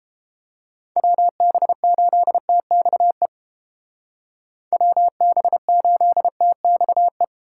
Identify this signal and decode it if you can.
Morse code, heard as a keyed tone, about 30 wpm, 700 Hz. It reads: WB8TXE WB8TXE